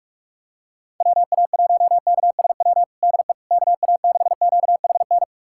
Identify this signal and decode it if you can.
Morse code, heard as a keyed tone, about 46 wpm, 700 Hz. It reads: WA1KSW DE KA6QHN
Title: USA Callsign DE Callsign - 6 Characters - 45wpm